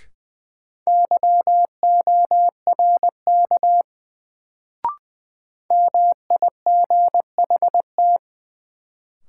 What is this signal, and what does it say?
Morse code, heard as a keyed tone, about 20 wpm, 700 Hz. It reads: YORK MIGHT
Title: Sets of 2 Words - New Word Drill - 20wpm